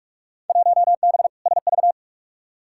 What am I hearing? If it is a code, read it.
Morse code, heard as a keyed tone, about 48 wpm, 700 Hz. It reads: 1B SV
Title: Field Day Exchanges - 45wpm